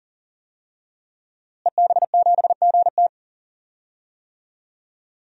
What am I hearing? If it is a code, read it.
Morse code, heard as a keyed tone, about 40 wpm, 700 Hz. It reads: EB7GT
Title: DXCC-100 Call Signs - 5 Characters - Repeated 2x - 40wpm